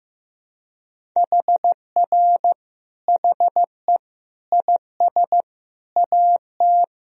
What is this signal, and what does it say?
Morse code, heard as a keyed tone, about 16 wpm, 700 Hz. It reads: HR HE IS AT
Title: Sentences from Top 300 Words and Abbreviations 15wpm